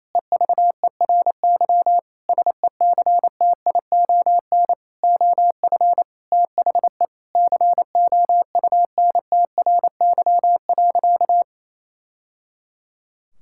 Morse code, heard as a keyed tone, about 28 wpm, 700 Hz. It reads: EVERY SECTION OF THE COUNTRY.
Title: Sentences from Top 1000 Words - Review Part 5 - 28wpm